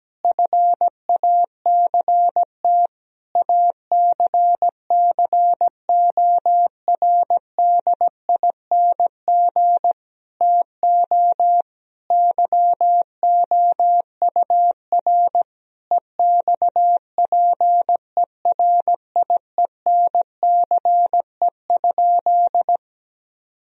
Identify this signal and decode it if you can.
Morse code, heard as a keyed tone, about 17 wpm, 700 Hz. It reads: FACT ACCORDING TO YOUR EXPERIENCE?
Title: Sentences from Top 1000 Words - Review Part 4 - 17wpm